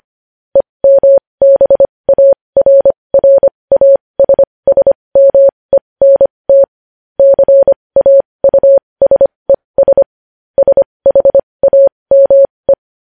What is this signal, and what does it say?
Morse code, heard as a keyed tone, about 25 wpm, spaced as for 20 wpm, 550 Hz. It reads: EMBARRASSMENT CAUSES SHAME
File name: podcast / News headlines-1x-25x20